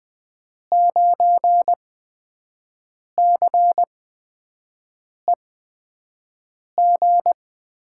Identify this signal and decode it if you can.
Morse code, heard as a keyed tone, about 20 wpm, spaced as for 5 wpm, 700 Hz. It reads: 9CEG
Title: USA Call Signs - 8x Character Spacing - 20wpm